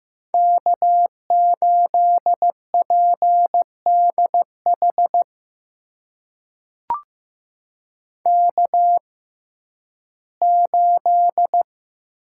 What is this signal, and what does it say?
Morse code, heard as a keyed tone, about 15 wpm, spaced as for 11 wpm, 700 Hz. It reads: K8PDH K8
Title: USA Call Signs - 5 Characters - 6x Character Spacing - 15wpm